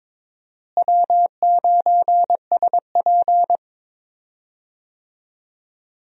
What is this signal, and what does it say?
Morse code, heard as a keyed tone, about 22 wpm, 700 Hz. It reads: W9SP
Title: USA Call Signs - 4 Characters - Repeated 3x - 22wpm